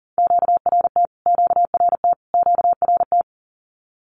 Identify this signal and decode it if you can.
Morse code, heard as a keyed tone, about 40 wpm, 700 Hz. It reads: QRT QRT QRT
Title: Q-Codes Repeated 3x 40wpm